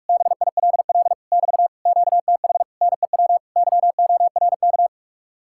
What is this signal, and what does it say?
Morse code, heard as a keyed tone, about 57 wpm, 700 Hz. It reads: BILL <BT> QTH NEW YORK